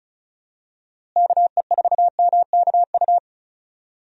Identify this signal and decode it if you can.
Morse code, heard as a keyed tone, about 34 wpm, 700 Hz. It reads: KE4MKU
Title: USA Call Signs - 6 Characters - Repeated 3x - 35wpm